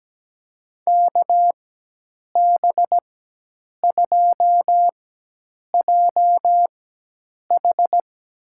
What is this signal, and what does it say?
Morse code, heard as a keyed tone, about 17 wpm, spaced as for 8 wpm, 700 Hz. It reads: KB2JH
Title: USA Call Signs - 4x Character Spacing - 17wpm